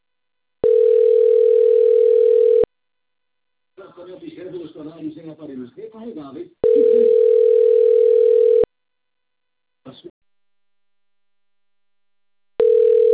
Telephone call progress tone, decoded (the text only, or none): ringback tone